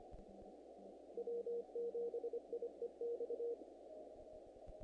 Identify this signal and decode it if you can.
Morse code, heard as a keyed tone, about 26 wpm, 470 Hz. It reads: W7IEX
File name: contests / naqp-cw-k5tr-op-k5pi- -1900